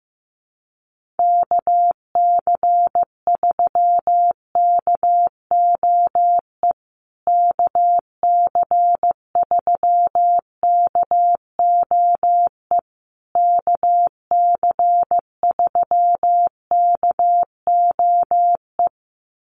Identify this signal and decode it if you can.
Morse code, heard as a keyed tone, about 15 wpm, 700 Hz. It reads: KC3KOE KC3KOE KC3KOE
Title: USA Call Signs Repeated 3x 15wpm